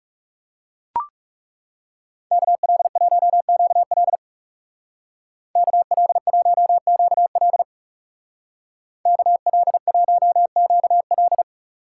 Morse code, heard as a keyed tone, about 36 wpm, 700 Hz. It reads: KL1QL KL1QL KL1QL
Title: Speed Racing - USA 2x2 Call Signs 30wpm